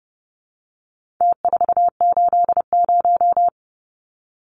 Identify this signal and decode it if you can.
Morse code, heard as a keyed tone, about 30 wpm, 700 Hz. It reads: T480
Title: Numbers - Contest Style - 4-Digit 30wpm